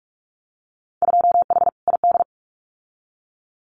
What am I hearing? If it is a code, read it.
Morse code, heard as a keyed tone, about 44 wpm, 700 Hz. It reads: 2H ID
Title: Winter Field Day Exchanges 45wpm